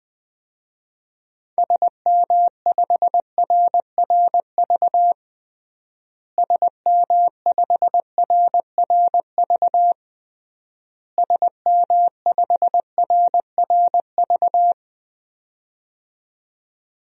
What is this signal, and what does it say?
Morse code, heard as a keyed tone, about 20 wpm, 700 Hz. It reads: SM5RRV SM5RRV SM5RRV
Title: DXCC-100 Call Signs - 6 Characters - Repeated 3x - 20wpm